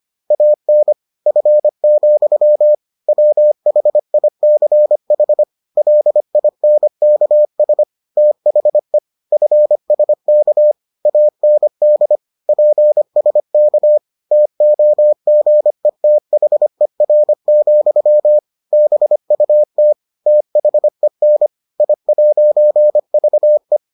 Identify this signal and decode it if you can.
Morse code, heard as a keyed tone, about 25 wpm, 600 Hz. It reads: AN F, WHICH LINKS THE FSK AND PSK TOGETHER, BUT THEN I'VE